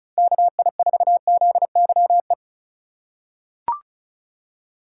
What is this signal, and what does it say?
Morse code, heard as a keyed tone, about 36 wpm, 700 Hz. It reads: KI4ZYE
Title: USA Call Signs - 3x Character Spacing - 35wpm